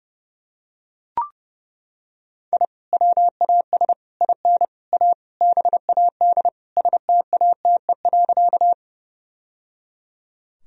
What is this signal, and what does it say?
Morse code, heard as a keyed tone, about 30 wpm, 700 Hz. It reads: I WAS IN A BAD STATE.
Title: Sentences from Top 1000 Words - Review Part 2 - 30wpm